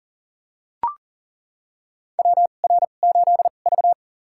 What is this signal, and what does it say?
Morse code, heard as a keyed tone, about 40 wpm, spaced as for 29 wpm, 700 Hz. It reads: WR8V
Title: USA Call Signs - 4 Characters - 2x Character Spacing - 40wpm